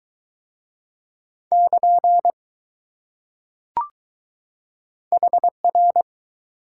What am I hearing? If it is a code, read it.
Morse code, heard as a keyed tone, about 20 wpm, 700 Hz. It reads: <KN> HR